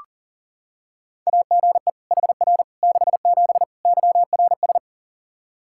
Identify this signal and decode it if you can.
Morse code, heard as a keyed tone, about 37 wpm, 700 Hz. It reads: AGE HR 67 YRS
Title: Practice QSO - Exchange 3 - More Info - 40wpm